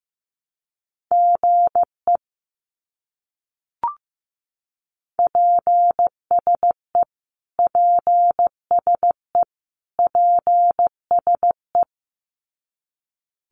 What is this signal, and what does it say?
Morse code, heard as a keyed tone, about 15 wpm, 700 Hz. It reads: GE PSE PSE PSE
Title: Abbreviations Repeated 3x - 15wpm